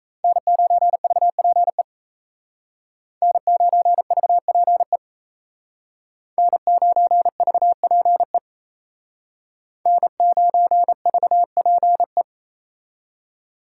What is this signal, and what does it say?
Morse code, heard as a keyed tone, about 34 wpm, 700 Hz. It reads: N9VPE N9VPE N9VPE N9VPE
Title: Speed Racing - USA 1x3 Call Signs 28wpm